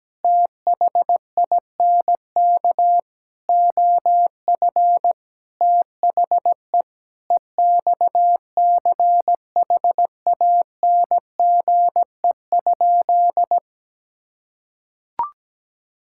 Morse code, heard as a keyed tone, about 17 wpm, 700 Hz. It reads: THINK OF THE EXCHANGE?
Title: Sentences from Top 1000 Words - Review Part 6 - 17wpm